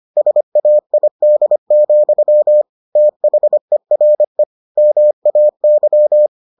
Morse code, heard as a keyed tone, about 25 wpm, 600 Hz. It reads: SAID, THERE MAY